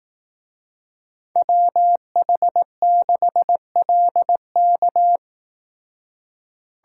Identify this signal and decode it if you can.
Morse code, heard as a keyed tone, about 18 wpm, 700 Hz. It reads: WH6LK
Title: Speed Racing - USA 2x2 Call Signs 15wpm